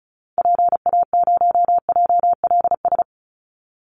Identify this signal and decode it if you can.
Morse code, heard as a keyed tone, about 34 wpm, 700 Hz. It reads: PA0JLS
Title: Call Signs Repeated 3x - 35wpm